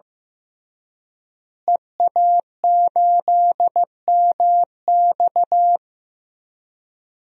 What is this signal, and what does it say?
Morse code, heard as a keyed tone, about 15 wpm, 700 Hz. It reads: EA8MX
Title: DXCC-100 Call Signs - 5 Characters - Repeated 3x - 15wpm